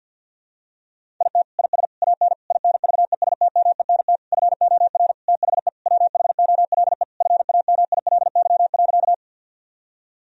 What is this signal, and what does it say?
Morse code, heard as a keyed tone, about 50 wpm, 700 Hz. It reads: IT IS AN INVESTMENT FOR THE WHOLE FAMILY.
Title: Sentences from Top 1000 Words - Review Part 4 - 50wpm